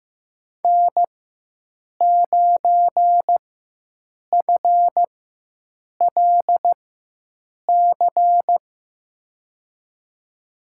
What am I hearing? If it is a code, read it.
Morse code, heard as a keyed tone, about 15 wpm, spaced as for 7 wpm, 700 Hz. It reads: N9FLC
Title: USA Call Signs - 5 Characters - 4x Character Spacing - 15wpm